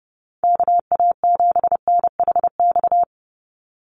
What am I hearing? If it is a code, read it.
Morse code, heard as a keyed tone, about 30 wpm, 700 Hz. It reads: KA7NHX
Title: USA Call Signs 30wpm